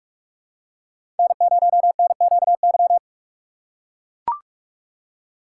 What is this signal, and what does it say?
Morse code, heard as a keyed tone, about 43 wpm, 700 Hz. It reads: N0NQY
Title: USA Call Signs - 5 Characters - 6x Character Spacing - 45wpm